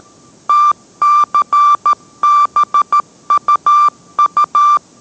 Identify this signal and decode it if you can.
Morse code, heard as a keyed tone, about 13 wpm, 1180 Hz. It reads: TCBUU